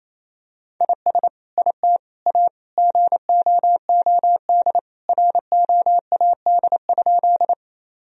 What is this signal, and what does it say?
Morse code, heard as a keyed tone, about 27 wpm, 700 Hz. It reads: IS IT A GOOD ROAD?
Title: Sentences from Top 1000 Words - Review Part 2 - 28wpm